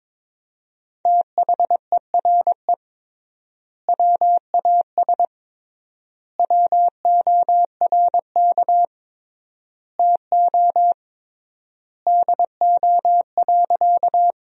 Morse code, heard as a keyed tone, about 22 wpm, 700 Hz. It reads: THERE WAS WORK TO DO.
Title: 3x Word-Spacing - Sentences from Top 200 Words - 22wpm